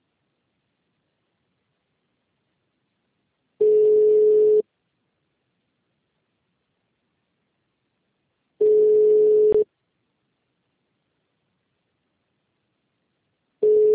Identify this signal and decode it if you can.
Ringback tone (Europe, 425 Hz).